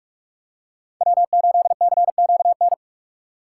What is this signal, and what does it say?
Morse code, heard as a keyed tone, about 46 wpm, 700 Hz. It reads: W8CQN
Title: USA Call Signs - 5 Characters - 4x Character Spacing - 45wpm